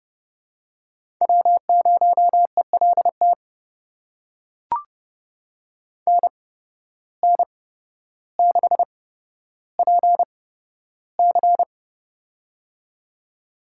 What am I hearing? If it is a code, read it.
Morse code, heard as a keyed tone, about 29 wpm, spaced as for 13 wpm, 700 Hz. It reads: W0ELT NN6PC
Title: USA Call Signs - 8x Character Spacing - 30wpm